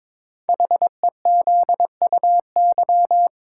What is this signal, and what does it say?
Morse code, heard as a keyed tone, about 23 wpm, 700 Hz. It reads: HEZUY
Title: Code Groups - 5 Letters - 22wpm